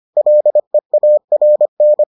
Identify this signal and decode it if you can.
Morse code, heard as a keyed tone, about 25 wpm, 600 Hz. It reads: LEARN